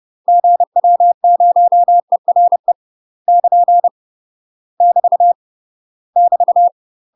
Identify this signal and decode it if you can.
Morse code, heard as a keyed tone, about 30 wpm, 700 Hz. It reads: GW0ERE <KN> <BT> <BT>
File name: UK MASTER QSOs TRAINING / 30WPM / UK MASTER QSOs 0005